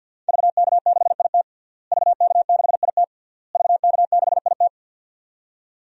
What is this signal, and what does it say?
Morse code, heard as a keyed tone, about 52 wpm, 700 Hz. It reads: VK6IT VK6IT VK6IT
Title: DXCC-100 Call Signs - 5 Characters - Repeated 3x - 50wpm